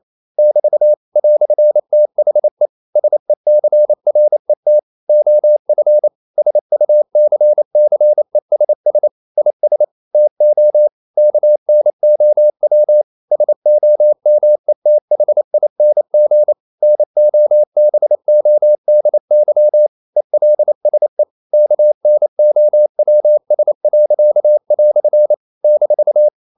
Morse code, heard as a keyed tone, about 28 wpm, 600 Hz. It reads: <BT> "THE SECRET OF SUCCESS IS TO KNOW SOMETHING NOBODY ELSE KNOWS." -
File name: CW / Practice / EnglishQuotations0228